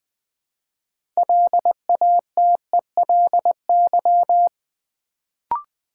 Morse code, heard as a keyed tone, about 20 wpm, 690 Hz. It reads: LATELY
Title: Build A Word - 2 Letters At A Time - 6-Letter Words - 20wpm